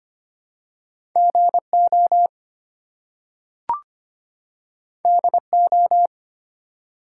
Morse code, heard as a keyed tone, about 25 wpm, 700 Hz. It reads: GO DO
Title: Two Letter Words - 25wpm